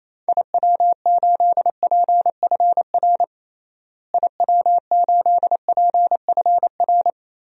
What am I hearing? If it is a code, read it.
Morse code, heard as a keyed tone, about 28 wpm, 700 Hz. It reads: IW8PFR IW8PFR
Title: DXCC-100 Call Signs - 6 Characters - Repeated 3x - 28wpm